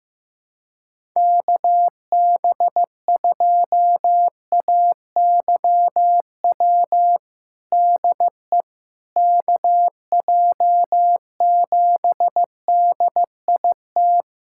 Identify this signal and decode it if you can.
Morse code, heard as a keyed tone, about 15 wpm, 700 Hz. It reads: KB2AYW DE KJ7DIT
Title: USA Callsign DE Callsign - 6 Characters - 15wpm